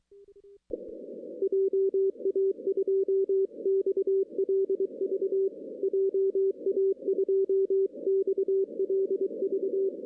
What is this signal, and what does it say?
Morse code, heard as a keyed tone, about 23 wpm, 390 Hz. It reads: JA2XLV JA2XLV
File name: 2016 arrl dx cw / 2016-ARRL-DX-CW-K5TR- - 0-25